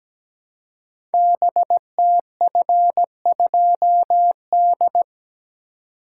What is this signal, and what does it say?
Morse code, heard as a keyed tone, about 17 wpm, 700 Hz. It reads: BTF2D
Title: Code Groups - 5 Letter:Numbers - 17wpm